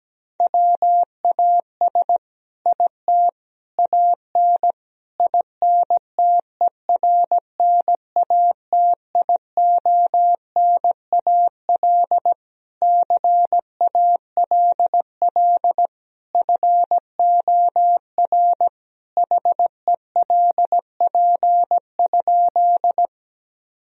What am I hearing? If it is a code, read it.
Morse code, heard as a keyed tone, about 17 wpm, 700 Hz. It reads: WAS IT AN INTERNATIONAL CALL FOR HELP?
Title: Sentences from Top 1000 Words - Review Part 4 - 17wpm